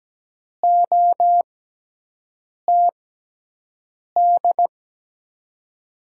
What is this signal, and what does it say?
Morse code, heard as a keyed tone, about 17 wpm, spaced as for 6 wpm, 700 Hz. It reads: OTD
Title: USA Call Signs - 6 Characters - 6x Character Spacing - 17wpm